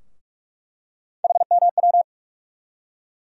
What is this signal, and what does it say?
Morse code, heard as a keyed tone, about 42 wpm, 700 Hz. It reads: HMW
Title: USA Call Sign Postfixes - 3 Letters 45wpm